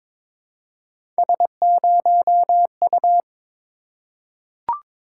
Morse code, heard as a keyed tone, about 23 wpm, 700 Hz. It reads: S0U
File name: practice / 22wpm / Three Letter-Number Combinations - 22wpm